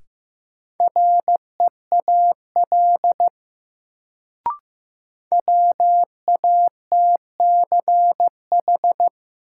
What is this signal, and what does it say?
Morse code, heard as a keyed tone, about 15 wpm, 700 Hz. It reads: REAL WATCH
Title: Sentences from Top 300 Words - New Word Drill - 15wpm